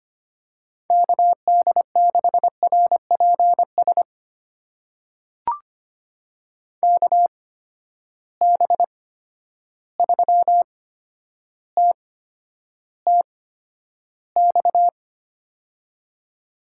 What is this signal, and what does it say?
Morse code, heard as a keyed tone, about 25 wpm, spaced as for 11 wpm, 700 Hz. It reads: KD6RPS KB3TTX
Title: USA Call Signs - 6 Characters - 8x Character Spacing - 25wpm